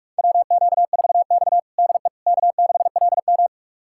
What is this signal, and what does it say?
Morse code, heard as a keyed tone, about 46 wpm, 700 Hz. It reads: WQ4X DE K6LM
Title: USA Callsign DE Callsign - 4 Characters - 45wpm